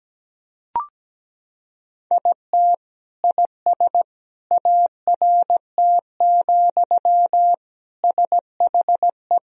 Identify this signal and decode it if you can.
Morse code, heard as a keyed tone, about 17 wpm, 700 Hz. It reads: IT IS ART, SHE
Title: Sentences from Top 1000 Words - Review Part 3 - 17wpm